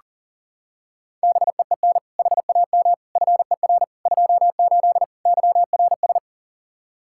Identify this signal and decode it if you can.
Morse code, heard as a keyed tone, about 37 wpm, 700 Hz. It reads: BEEN HAM FER 28 YRS
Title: K1USN SST QSO Practice 40wpm